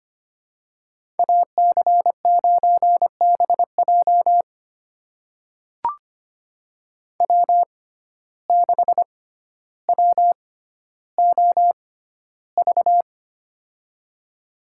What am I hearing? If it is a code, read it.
Morse code, heard as a keyed tone, about 25 wpm, spaced as for 13 wpm, 700 Hz. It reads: AC9BJ W6WOV
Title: USA Call Signs - 5 Characters - 6x Character Spacing - 25wpm